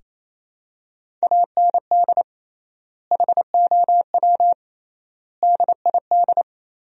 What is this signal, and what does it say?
Morse code, heard as a keyed tone, about 28 wpm, 700 Hz. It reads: AND HOW DID
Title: 3x Word-Spacing - Sentences from Top 200 Words - 28wpm